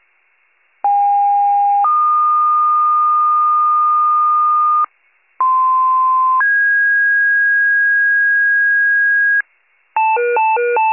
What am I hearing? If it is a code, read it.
Two-tone page, heard as a one-second tone, then a three-second tone, about 810 then 1220 Hz, about 1040 then 1650 Hz, with noise. It is heard from 0.8 to 9.4 s.